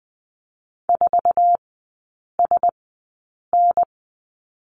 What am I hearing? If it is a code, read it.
Morse code, heard as a keyed tone, about 20 wpm, spaced as for 8 wpm, 700 Hz. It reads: 4SN
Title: 074 Lesson 13e - Letter L - Three Characters 20wpm